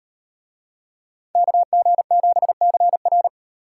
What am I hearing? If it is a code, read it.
Morse code, heard as a keyed tone, about 36 wpm, 700 Hz. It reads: KG7CR